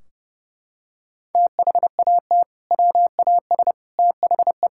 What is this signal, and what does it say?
Morse code, heard as a keyed tone, about 30 wpm, 700 Hz. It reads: THAT WAS THE
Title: Sentences from Top 1600 Words 30wpm